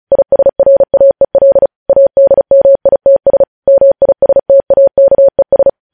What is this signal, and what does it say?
Morse code, heard as a keyed tone, about 36 wpm, 550 Hz. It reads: ISRAEL ADMITS MISTAKES
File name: podcast / News headlines-1x-35x35